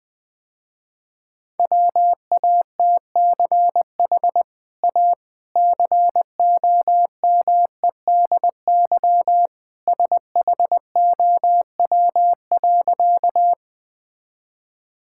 Morse code, heard as a keyed tone, about 20 wpm, 700 Hz. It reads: WATCH A COMEDY SHOW.
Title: Easy Sentences with Hints - 20wpm